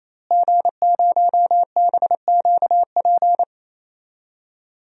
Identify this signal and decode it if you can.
Morse code, heard as a keyed tone, about 27 wpm, 700 Hz. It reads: G0BQP
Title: DXCC-100 Call Signs - 5 Characters - Repeated 3x - 28wpm